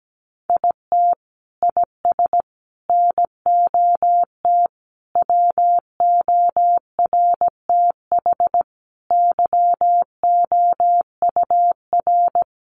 Morse code, heard as a keyed tone, about 17 wpm, 700 Hz. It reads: IT IS NOT WORTH YOUR
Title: Sentences from Top 800 Words 17wpm